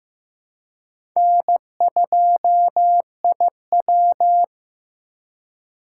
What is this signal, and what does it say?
Morse code, heard as a keyed tone, about 15 wpm, 700 Hz. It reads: N2IW
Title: USA Call Signs - 4 Characters - 3x Character Spacing - 15wpm